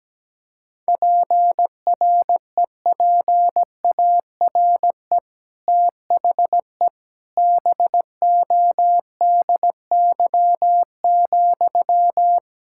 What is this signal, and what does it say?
Morse code, heard as a keyed tone, about 17 wpm, 700 Hz. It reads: PREPARE THE BODY,
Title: Sentences from Top 1000 Words - Review Part 6 - 17wpm